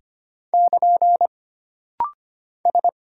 Morse code, heard as a keyed tone, about 25 wpm, 700 Hz. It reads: <KN> S
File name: Single Character (Letter:Number:Punctuation:Prosign) - Rapid-Fire - 25wpm